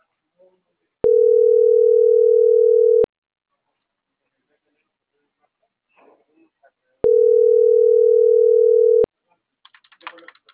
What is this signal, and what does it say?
Ringback tone (North America), 440 + 480 Hz.